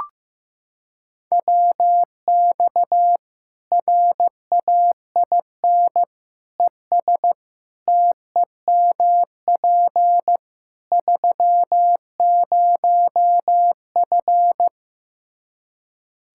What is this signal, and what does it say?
Morse code, heard as a keyed tone, about 15 wpm, 700 Hz. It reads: WX RAIN ES TEMP 30F
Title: K1USN SST QSO Practice 15wpm